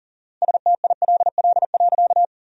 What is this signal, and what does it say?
Morse code, heard as a keyed tone, about 40 wpm, 700 Hz. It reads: STILL.
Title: 4x Word-Spacing - Sentences from Top 500 Words - 40wpm